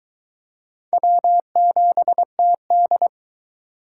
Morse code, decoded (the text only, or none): W7TD